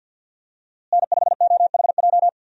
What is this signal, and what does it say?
Morse code, heard as a keyed tone, about 52 wpm, 700 Hz. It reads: N5OHJ
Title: USA Call Signs - 5 Characters - Repeated 3x - 50wpm